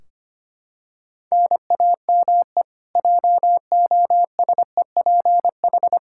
Morse code, heard as a keyed tone, about 25 wpm, 700 Hz. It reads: NAME JOSEPH
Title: K1USN SST QSO Practice 25wpm